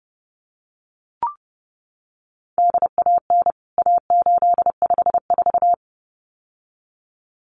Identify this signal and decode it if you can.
Morse code, heard as a keyed tone, about 30 wpm, 700 Hz. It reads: DAN A854
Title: CWT Exchange 30wpm